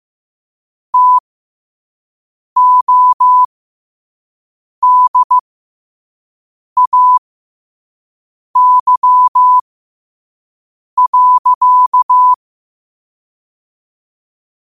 Morse code, heard as a keyed tone, about 15 wpm, spaced as for 5 wpm, 1000 Hz. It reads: TODAY.
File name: qotdfiles / 5wpm